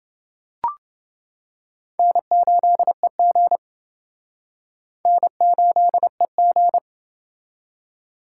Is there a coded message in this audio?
Yes: Morse code from 2.0 to 6.8 s, heard as a keyed tone, about 28 wpm, 700 Hz. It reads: N8EG N8EG